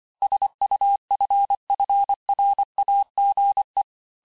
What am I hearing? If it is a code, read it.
Morse code, heard as a keyed tone, about 25 wpm, 800 Hz. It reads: SUFFRAGE
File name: cours-du-lundi files / cours-1595-teste-CLAIR-vitesse18-MPM-puis23MPM